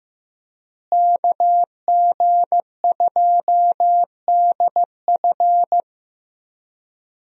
Morse code, heard as a keyed tone, about 15 wpm, 700 Hz. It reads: KG2DF
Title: USA Call Signs - 5 Characters - Repeated 3x - 15wpm